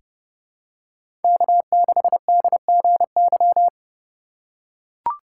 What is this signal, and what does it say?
Morse code, heard as a keyed tone, about 30 wpm, 700 Hz. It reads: K6DGY
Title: USA Call Signs - 5 Characters - 8x Character Spacing - 30wpm